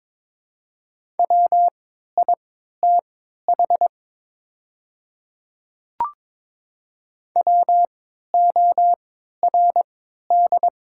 Morse code, heard as a keyed tone, about 23 wpm, spaced as for 13 wpm, 700 Hz. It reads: WITH WORD
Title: 3x Character Spacing - Top 100 Words - 22wpm